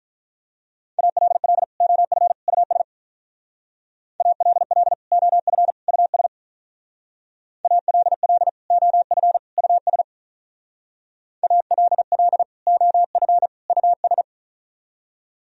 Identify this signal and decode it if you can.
Morse code, heard as a keyed tone, about 45 wpm, 700 Hz. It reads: ALL OF US ALL OF US ALL OF US ALL OF US